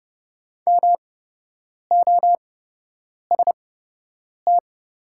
Morse code, heard as a keyed tone, about 30 wpm, spaced as for 8 wpm, 700 Hz. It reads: MOST
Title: 8x Character Spacing - Top 500 Words - 30wpm